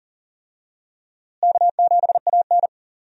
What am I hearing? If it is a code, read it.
Morse code, heard as a keyed tone, about 40 wpm, 700 Hz. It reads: K7AN
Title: USA Call Signs - 4 Characters - Repeated 2x - 40wpm